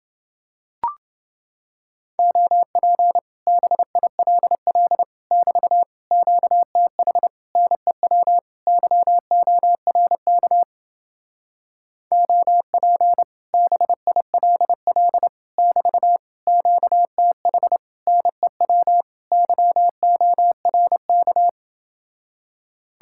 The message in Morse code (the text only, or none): OP BILL <BT> QTH NEW YORK OP BILL <BT> QTH NEW YORK